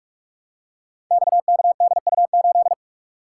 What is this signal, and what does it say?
Morse code, heard as a keyed tone, about 44 wpm, 700 Hz. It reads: XKDU8
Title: Code Groups - 5 Letter:Numbers - 45wpm